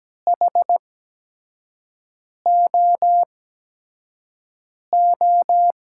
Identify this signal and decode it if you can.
Morse code, heard as a keyed tone, about 17 wpm, spaced as for 5 wpm, 700 Hz. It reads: HOO
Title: USA Call Signs - 8x Character Spacing - 17wpm